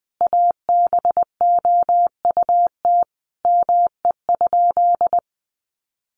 Morse code, heard as a keyed tone, about 20 wpm, 700 Hz. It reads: ABOUT ME?
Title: Sentences from Top 1500 Words 20wpm